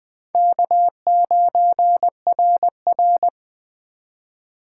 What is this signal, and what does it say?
Morse code, heard as a keyed tone, about 23 wpm, 700 Hz. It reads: K9RR
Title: Speed Racing - USA Call Signs 20wpm